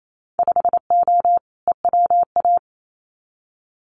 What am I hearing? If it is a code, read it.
Morse code, heard as a keyed tone, about 28 wpm, 700 Hz. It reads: 5O EWA
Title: Winter Field Day Exchanges Encore 28wpm